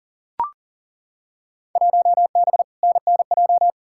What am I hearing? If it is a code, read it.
Morse code, heard as a keyed tone, about 40 wpm, 700 Hz. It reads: 1B NNJ
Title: Field Day Exchanges - 40wpm